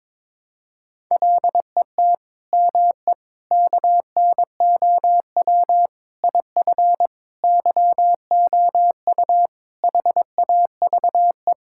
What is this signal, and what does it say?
Morse code, heard as a keyed tone, about 21 wpm, 700 Hz. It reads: LET ME KNOW IF YOU HAVE
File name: practice / 22wpm / Sets of 8 Words - 22wpm